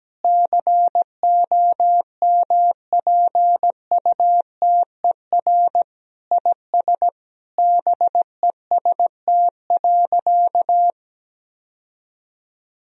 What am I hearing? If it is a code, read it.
Morse code, heard as a keyed tone, about 17 wpm, 700 Hz. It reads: COMPUTER IS BEST.
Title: Sentences from Top 1000 Words - Review Part 3 - 17wpm